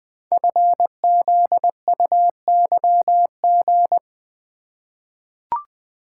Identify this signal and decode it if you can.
Morse code, heard as a keyed tone, about 20 wpm, 700 Hz. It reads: FZUYG
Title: Code Groups - 5 Letter:Numbers - 20wpm